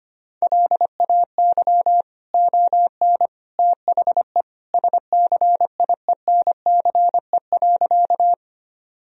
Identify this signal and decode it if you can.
Morse code, heard as a keyed tone, about 25 wpm, 700 Hz. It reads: LAY ON THE SCIENCE.
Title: Sentences from Top 1000 Words - Review Part 2 - 25wpm